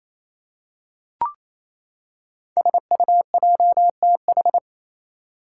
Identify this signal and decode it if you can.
Morse code, heard as a keyed tone, about 27 wpm, 700 Hz. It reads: SUJTH
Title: Code Groups - 5 Letter:Numbers - 28wpm